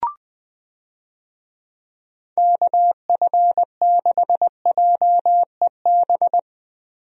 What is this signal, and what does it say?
Morse code, heard as a keyed tone, about 20 wpm, 700 Hz. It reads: KF6JEB
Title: USA Call Signs - 6 Characters - Repeated 2x - 20wpm